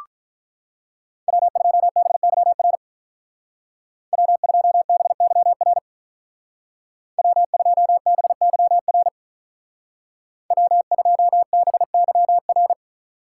Morse code, heard as a keyed tone, about 44 wpm, 700 Hz. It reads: W2BYR W2BYR W2BYR W2BYR